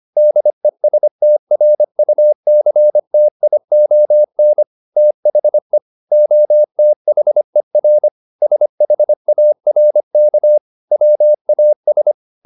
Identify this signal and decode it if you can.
Morse code, heard as a keyed tone, about 25 wpm, 600 Hz. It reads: DESTRUCTION THE OTHER SHARK WAS